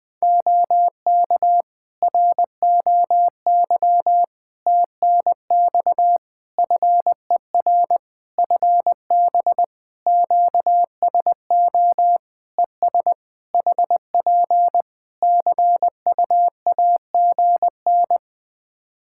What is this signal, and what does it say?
Morse code, heard as a keyed tone, about 20 wpm, 700 Hz. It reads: OK ROY TNX FER FB QSO ES HP CUAGN
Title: Practice QSO - Exchange 4 - Ending - 20wpm